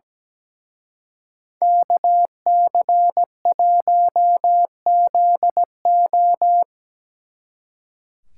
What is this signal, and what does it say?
Morse code, heard as a keyed tone, about 20 wpm, 700 Hz. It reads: KC1ZO
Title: Speed Racing - USA 2x2 Call Signs 17wpm